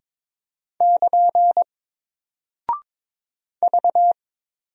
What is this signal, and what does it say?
Morse code, heard as a keyed tone, about 22 wpm, 700 Hz. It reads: <KN> V
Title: Single Character (Letter:Number:Punctuation:Prosign) Alt 22wpm